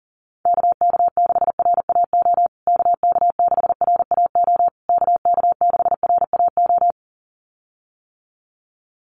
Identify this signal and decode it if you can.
Morse code, heard as a keyed tone, about 40 wpm, 700 Hz. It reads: KK6RAO KK6RAO KK6RAO
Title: USA Call Signs Repeated 3x 40wpm